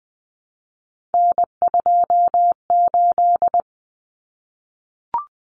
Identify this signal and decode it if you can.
Morse code, heard as a keyed tone, about 19 wpm, 700 Hz. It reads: N28
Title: Common Cut Numbers - 3-Digit 20wpm